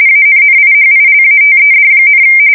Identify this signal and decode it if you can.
RTTY, about 45.45 baud. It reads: ANNIHILATED,
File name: fsk170 / USS-New-Jersey